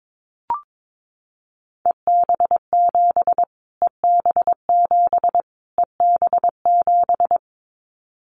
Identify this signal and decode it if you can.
Morse code, heard as a keyed tone, about 22 wpm, 700 Hz. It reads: EB7 EB7 EB7
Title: Callsign Prefixes for Top 25 DXCC - Repeated 3x - 22wpm